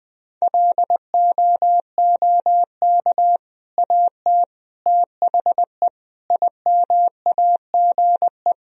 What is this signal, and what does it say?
Morse code, heard as a keyed tone, about 20 wpm, 700 Hz. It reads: LOOK AT THE IMAGE
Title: Sentences from Top 1000 Words - Review Part 4 - 20wpm